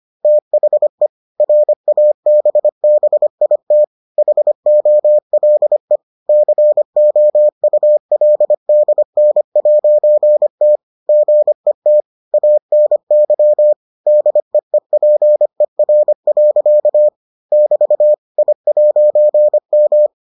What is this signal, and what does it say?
Morse code, heard as a keyed tone, about 25 wpm, 600 Hz. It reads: THE RABBIT HOLE COULDN'T GET ANY DEEPER. <BT> I'M